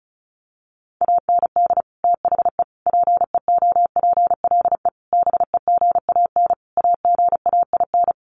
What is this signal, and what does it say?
Morse code, heard as a keyed tone, about 34 wpm, 700 Hz. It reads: AND THE PEOPLE BEGAN AGAIN
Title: Sentences from Top 1500 Words 35wpm